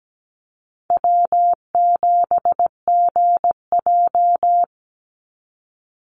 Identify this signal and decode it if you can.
Morse code, heard as a keyed tone, about 17 wpm, 700 Hz. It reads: W7GJ
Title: NA SOTA Chaser Callsigns 17wpm